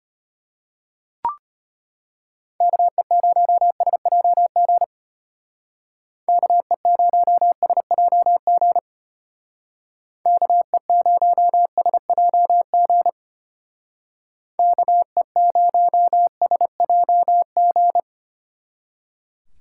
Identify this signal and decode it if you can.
Morse code, heard as a keyed tone, about 30 wpm, 700 Hz. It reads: KE0SJG KE0SJG KE0SJG KE0SJG
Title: Speed Racing - USA Call Signs 25wpm